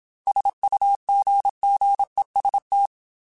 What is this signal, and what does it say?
Morse code, heard as a keyed tone, about 27 wpm, 800 Hz. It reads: SUGGEST
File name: Technical Subjects / Technical Subjects-mp3s / AND Gate @25